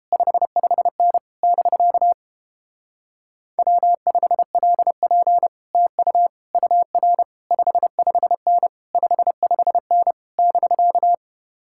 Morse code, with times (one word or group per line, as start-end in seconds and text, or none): 0.1-1.2 55N
1.4-2.1 <BK>
3.6-5.5 W5LP
5.8-6.3 TU
6.5-7.2 UR
7.5-8.7 55N
9.0-10.1 55N
10.4-11.1 <BK>